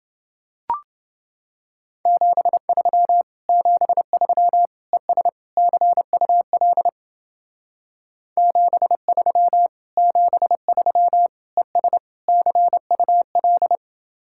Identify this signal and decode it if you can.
Morse code, heard as a keyed tone, about 29 wpm, 700 Hz. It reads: 73 73 ES CUL 73 73 ES CUL